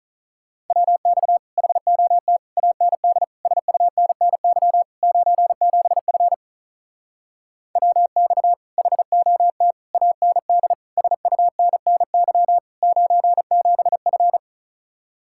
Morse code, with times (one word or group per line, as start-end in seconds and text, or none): 0.7-1.4 WX
1.6-2.4 HOT
2.6-3.2 AND
3.5-4.8 SUNNY
5.0-6.3 97F
7.8-8.5 WX
8.8-9.7 HOT
9.9-10.7 AND
11.0-12.6 SUNNY
12.8-14.4 97F